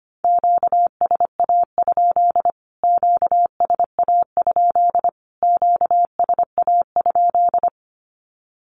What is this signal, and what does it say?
Morse code, heard as a keyed tone, about 25 wpm, 700 Hz. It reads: QSA? QSA? QSA?
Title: Q-Codes Repeated 3x 25wpm